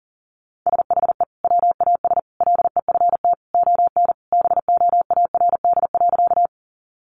Morse code, heard as a keyed tone, about 40 wpm, 700 Hz. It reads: SHE WAS LEFT ON BOARD.
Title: Sentences from Top 700 Words 40wpm